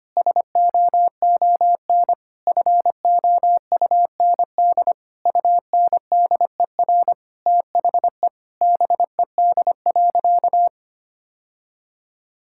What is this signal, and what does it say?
Morse code, heard as a keyed tone, about 25 wpm, 700 Hz. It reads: SOON FOUND UNDER THE BED.
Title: Sentences from Top 1000 Words - Review Part 5 - 25wpm